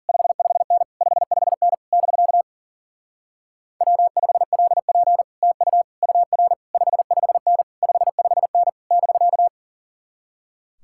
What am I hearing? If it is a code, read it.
Morse code, heard as a keyed tone, about 52 wpm, 700 Hz. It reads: 55N 55N <BK> W5LP TU UR 55N 55N <BK>